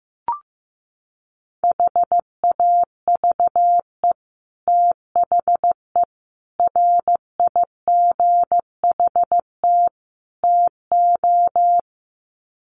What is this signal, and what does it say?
Morse code, heard as a keyed tone, about 15 wpm, 700 Hz. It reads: HAVE THE RIGHT TO
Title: Sets of 4 Words 15wpm